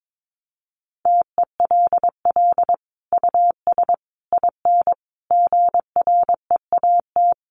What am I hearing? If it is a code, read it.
Morse code, heard as a keyed tone, about 22 wpm, 700 Hz. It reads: TELL US IN GREAT
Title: Sentences from Top 1500 Words 22wpm